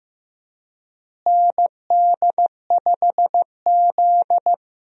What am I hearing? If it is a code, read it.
Morse code, heard as a keyed tone, about 15 wpm, 700 Hz. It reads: ND5Z
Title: USA Call Signs - 4 Characters - Repeated 2x - 15wpm